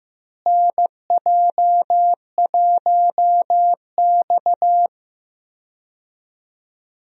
Morse code, heard as a keyed tone, about 16 wpm, 700 Hz. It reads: NJ1X
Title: USA Call Signs - 4 Characters - Repeated 3x - 15wpm